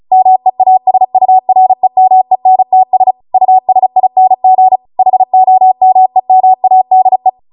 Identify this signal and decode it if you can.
Morse code, heard as a keyed tone, about 36 wpm, 750 Hz. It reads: MEASUREMENTS USING HOMEMADE